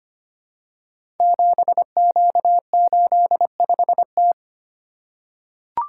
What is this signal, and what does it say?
Morse code, heard as a keyed tone, about 25 wpm, 700 Hz. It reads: 7Q85T
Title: Code Groups - 5 Letter:Numbers - 25wpm